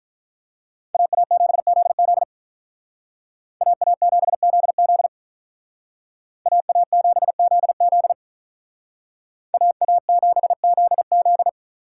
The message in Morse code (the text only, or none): AA7ZZ AA7ZZ AA7ZZ AA7ZZ